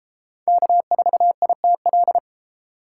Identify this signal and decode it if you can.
Morse code, heard as a keyed tone, about 31 wpm, 700 Hz. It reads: K4ITL